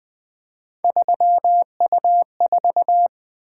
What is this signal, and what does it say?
Morse code, heard as a keyed tone, about 20 wpm, 700 Hz. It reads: 3U4
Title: Three Letter-Number Combinations - 20wpm